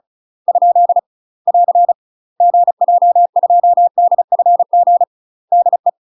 Morse code, heard as a keyed tone, about 36 wpm, 700 Hz. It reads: ? <AR> GJ2DFG DE
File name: UK MASTER QSOs TRAINING / 35WPM / UK MASTER QSOs 0017